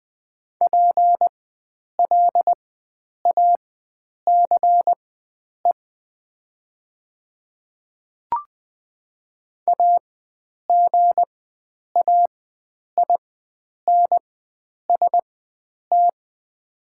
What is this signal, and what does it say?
Morse code, heard as a keyed tone, about 20 wpm, spaced as for 9 wpm, 700 Hz. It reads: PLACE AGAINST
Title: 4x Character Spacing - Top 500 Words - 20wpm